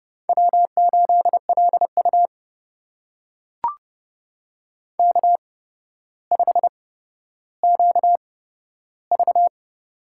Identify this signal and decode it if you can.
Morse code, heard as a keyed tone, about 30 wpm, spaced as for 13 wpm, 700 Hz. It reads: W8LU K5QV
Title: USA Call Signs - 4 Characters - 8x Character Spacing - 30wpm